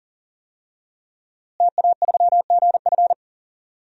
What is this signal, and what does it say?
Morse code, heard as a keyed tone, about 40 wpm, 700 Hz. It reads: TA3GF
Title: DXCC-100 Call Signs - 5 Characters - Repeated 3x - 40wpm